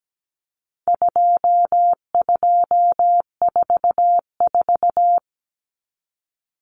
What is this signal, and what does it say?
Morse code, heard as a keyed tone, about 17 wpm, 700 Hz. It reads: 2244
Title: Common Cut Numbers - 4-Digit 17wpm